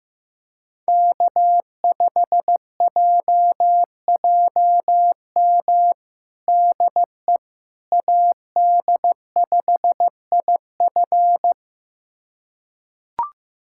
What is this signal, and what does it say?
Morse code, heard as a keyed tone, about 15 wpm, 700 Hz. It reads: K5JJM DE AD5IF
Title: USA Callsign DE Callsign - 5 Characters - 15wpm